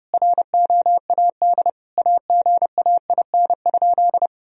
Morse code, heard as a keyed tone, about 30 wpm, 700 Hz. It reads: ROAD AGAIN?
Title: Sentences from Top 1000 Words - Review Part 2 - 30wpm